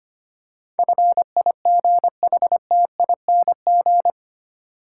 Morse code, heard as a keyed tone, about 25 wpm, 700 Hz. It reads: FIGHTING
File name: Eight Letter Words - 25wpm